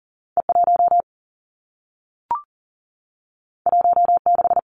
Morse code, heard as a keyed tone, about 42 wpm, 700 Hz. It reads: E 1 1 6
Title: Two Challenging Letter-Number Combinations 40wpm